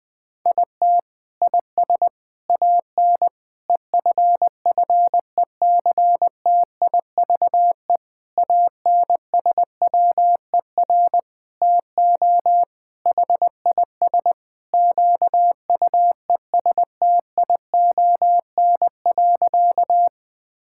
Morse code, heard as a keyed tone, about 20 wpm, 700 Hz. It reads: IT IS AN EFFECTIVE ANSWER TO HIS QUESTION.
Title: Sentences from Top 1000 Words - Review Part 6 - 20wpm